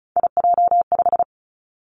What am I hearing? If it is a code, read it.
Morse code, heard as a keyed tone, about 36 wpm, 700 Hz. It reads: IJ5
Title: Three Challenging Letter-Number Combinations 35wpm